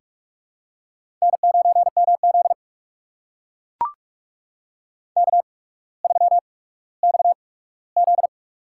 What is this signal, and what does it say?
Morse code, heard as a keyed tone, about 44 wpm, spaced as for 19 wpm, 700 Hz. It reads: N9MZ K3XZ
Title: USA Call Signs - 4 Characters - 8x Character Spacing - 45wpm